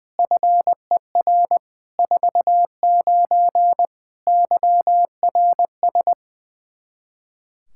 Morse code, heard as a keyed tone, about 20 wpm, 700 Hz. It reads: FER 49 YRS
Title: Practice QSO - Exchange 3 - More Info - 20wpm